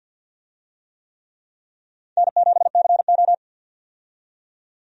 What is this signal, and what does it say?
Morse code, heard as a keyed tone, about 47 wpm, 700 Hz. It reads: N7CO
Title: USA Call Signs - 4 Characters - 50wpm